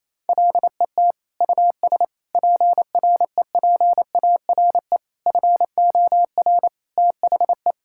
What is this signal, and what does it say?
Morse code, heard as a keyed tone, about 27 wpm, 700 Hz. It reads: LET US PREPARE FOR THE
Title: Sentences from Top 1000 Words - Review Part 6 - 28wpm